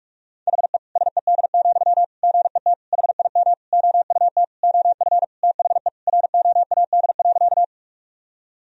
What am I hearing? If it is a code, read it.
Morse code, heard as a keyed tone, about 43 wpm, 700 Hz. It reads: HE SED, GET HIM OUT OF THE ROAD.
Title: Sentences from Top 400 Words and Abbreviations 45wpm